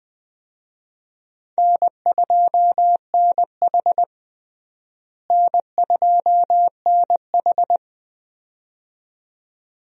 Morse code, heard as a keyed tone, about 20 wpm, 700 Hz. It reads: N2NH N2NH
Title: USA Call Signs - 4 Characters - Repeated 2x - 20wpm